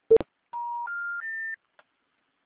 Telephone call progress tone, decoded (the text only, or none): special information tone